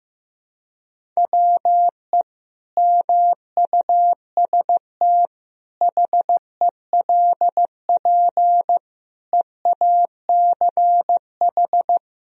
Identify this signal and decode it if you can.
Morse code, heard as a keyed tone, about 15 wpm, 700 Hz. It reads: WE MUST HELP EACH
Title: Sentences from Top 1000 Words - Review Part 1 - 15wpm